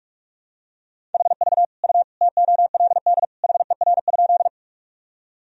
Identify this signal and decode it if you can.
Morse code, heard as a keyed tone, about 41 wpm, 700 Hz. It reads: HV U TOLD HER?
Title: Sentences from Top 400 Words and Abbreviations 45wpm